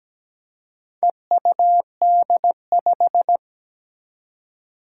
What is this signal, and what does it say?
Morse code, heard as a keyed tone, about 17 wpm, 700 Hz. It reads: EUD5
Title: Four Challenging Letter-Number Combinations 17wpm